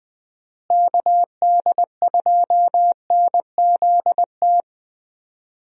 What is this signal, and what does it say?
Morse code, heard as a keyed tone, about 20 wpm, 700 Hz. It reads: KD2NZT
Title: USA Call Signs - 6 Characters - Repeated 2x - 20wpm